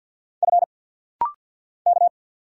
Morse code, heard as a keyed tone, about 49 wpm, 700 Hz. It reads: F K
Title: Single Letter-Number Alt - Rapid-Fire 50wpm